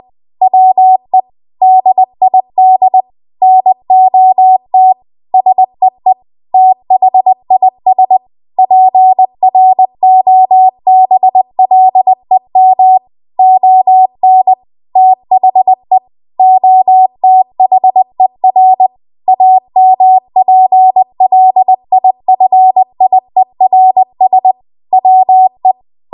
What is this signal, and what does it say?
Morse code, heard as a keyed tone, about 20 wpm, 750 Hz. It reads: WE DID NOT SEE THIS PROBLEM ON THE OTHER AMPLIFIERS WE
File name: Archive / 20 WPM / 20WPM